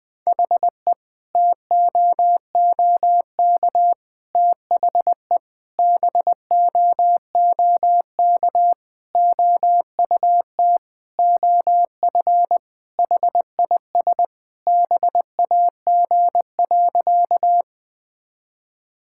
Morse code, heard as a keyed tone, about 21 wpm, 700 Hz. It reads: HE TOOK THE BOOK OUT OF HIS BAG.
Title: Sentences with 3-Word Phrases - 20wpm